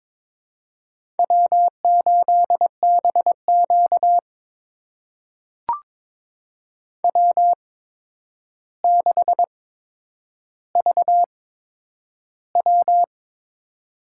Morse code, heard as a keyed tone, about 22 wpm, spaced as for 9 wpm, 700 Hz. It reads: W8BQ W6VW
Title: USA Call Signs - 8x Character Spacing - 22wpm